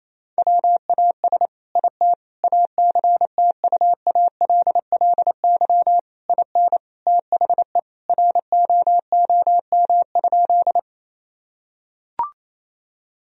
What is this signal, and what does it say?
Morse code, heard as a keyed tone, about 28 wpm, 700 Hz. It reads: WAS IT ACTUALLY IN THE ROOM?
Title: Sentences from Top 1000 Words - Review Part 4 - 28wpm